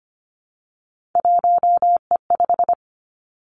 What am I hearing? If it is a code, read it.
Morse code, heard as a keyed tone, about 25 wpm, 700 Hz. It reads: 1E5
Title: Three Challenging Letter-Number Combinations 25wpm